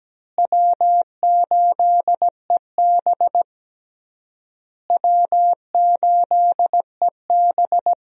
Morse code, heard as a keyed tone, about 17 wpm, 700 Hz. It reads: W8EB W8EB
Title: USA Call Signs - 4 Characters - Repeated 3x - 17wpm